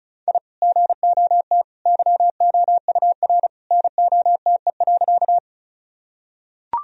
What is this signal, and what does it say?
Morse code, heard as a keyed tone, about 34 wpm, 700 Hz. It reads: I GOT YOUR NOTE.
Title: Sentences from Top 1000 Words - Review Part 2 - 35wpm